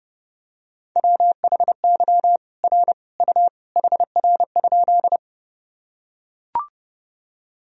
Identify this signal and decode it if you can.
Morse code, heard as a keyed tone, about 32 wpm, 700 Hz. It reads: WHY R U HR?
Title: Sentences from Top 200 Words and Abbreviations 30wpm